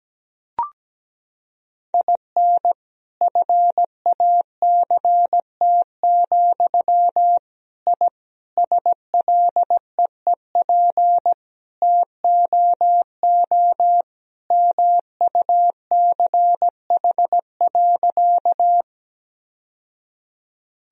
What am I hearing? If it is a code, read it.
Morse code, heard as a keyed tone, about 17 wpm, 700 Hz. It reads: IN FACT, I SLEEP TOO MUCH.
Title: Sentences from Top 1000 Words - Review Part 3 - 17wpm